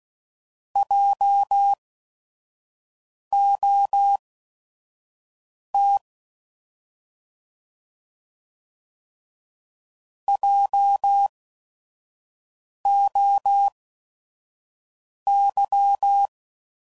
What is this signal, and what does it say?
Morse code, heard as a keyed tone, about 16 wpm, spaced as for 5 wpm, 800 Hz. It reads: JOT JOY